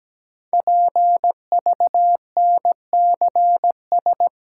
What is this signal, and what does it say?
Morse code, heard as a keyed tone, about 17 wpm, 700 Hz. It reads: PVNCS
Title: Code Groups - 5 Letter:Numbers - 17wpm